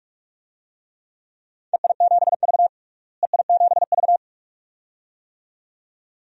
Morse code, heard as a keyed tone, about 46 wpm, 700 Hz. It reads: EI7V EI7V
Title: DXCC-100 Call Signs - 4 Characters - Repeated 2x - 45wpm